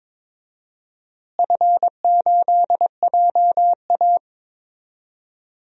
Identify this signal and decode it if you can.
Morse code, heard as a keyed tone, about 21 wpm, 700 Hz. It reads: F8JA
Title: DXCC-100 Call Signs - 4 Characters - Repeated 3x - 22wpm